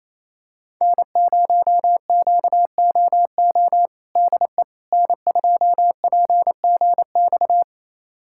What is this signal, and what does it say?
Morse code, heard as a keyed tone, about 28 wpm, 700 Hz. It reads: N0QOO DE N2PGX
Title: USA Callsign DE Callsign - 5 Characters - 28wpm